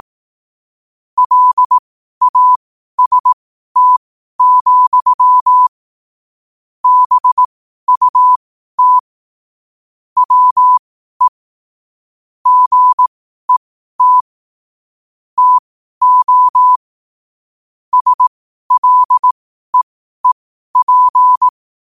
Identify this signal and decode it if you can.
Morse code, heard as a keyed tone, about 18 wpm, spaced as for 12 wpm, 1000 Hz. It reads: LAST, BUT WE GET TO SLEEP
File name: qotdfiles / 12wpm